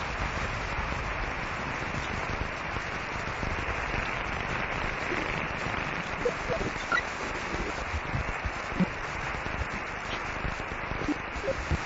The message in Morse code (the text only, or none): none